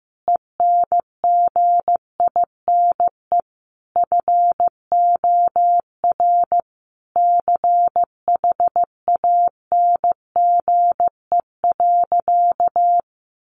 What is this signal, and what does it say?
Morse code, heard as a keyed tone, about 15 wpm, 700 Hz. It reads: ENGINE FOR CHANGE.
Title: Sentences from Top 1500 Words 15wpm